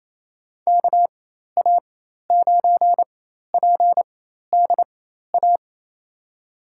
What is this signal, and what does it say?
Morse code, heard as a keyed tone, about 28 wpm, spaced as for 13 wpm, 700 Hz. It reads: KA9PDA
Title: USA Call Signs - 6 Characters - 4x Character Spacing - 28wpm